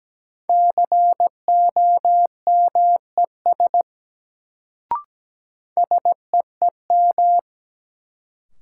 Morse code, heard as a keyed tone, about 17 wpm, 700 Hz. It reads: COMES SEEM
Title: Sets of 3 Words - New Word Drill - 17wpm